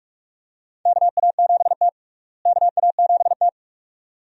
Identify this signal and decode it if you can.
Morse code, heard as a keyed tone, about 41 wpm, 700 Hz. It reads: KA7T KA7T
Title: USA Call Signs - 4 Characters - Repeated 2x - 45wpm